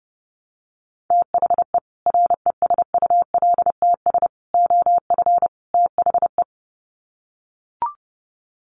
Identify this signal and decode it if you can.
Morse code, heard as a keyed tone, about 30 wpm, 700 Hz. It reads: THE RESULTS OF THE
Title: Sets of 4 Words 30wpm